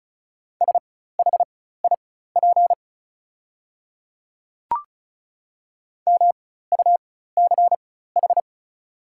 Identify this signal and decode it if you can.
Morse code, heard as a keyed tone, about 36 wpm, spaced as for 16 wpm, 690 Hz. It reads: SHIP MUCH
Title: 4x Character Spacing - Top 500 Words - 35wpm